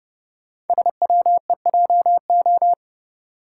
Morse code, heard as a keyed tone, about 30 wpm, 690 Hz. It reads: SWEJO
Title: Code Groups - 5 Letters - 30wpm